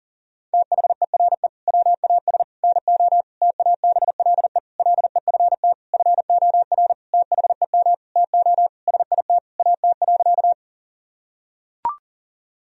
Morse code, heard as a keyed tone, about 40 wpm, 700 Hz. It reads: THERE WAS NO TABLE LEFT FOR THEM TO SIT AT.
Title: Sentences from Top 1000 Words - Review Part 2 - 40wpm